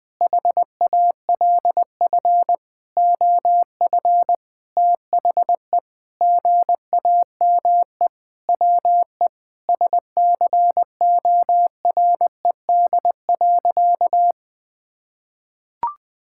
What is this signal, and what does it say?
Morse code, heard as a keyed tone, about 20 wpm, 700 Hz. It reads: HALF OF THE GAME WE SCORED.
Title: Sentences with 6-Word Phrases - Spoken Beginning - 20wpm